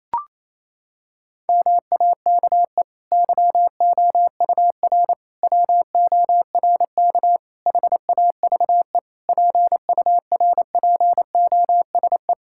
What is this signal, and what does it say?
Morse code, heard as a keyed tone, about 28 wpm, 700 Hz. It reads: MAKE YOUR WORK HAVE PURPOSE
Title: Sentences from Top 1000 Words - Review Part 6 - 28wpm